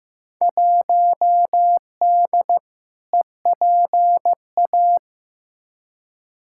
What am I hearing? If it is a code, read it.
Morse code, heard as a keyed tone, about 15 wpm, 700 Hz. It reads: 1D EPA
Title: Field Day Exchanges - 15wpm